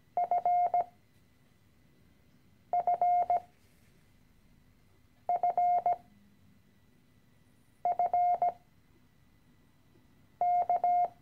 Morse code, heard as a keyed tone, about 17 wpm, spaced as for 4 wpm, 710 Hz. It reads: FFFFK
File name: CW-Online-Kurs / Anfänger / Lektion 013